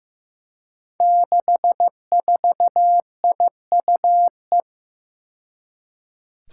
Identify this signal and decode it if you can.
Morse code, heard as a keyed tone, about 15 wpm, 700 Hz. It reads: 64IUE
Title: Five Challenging Letter-Number Combinations 15wpm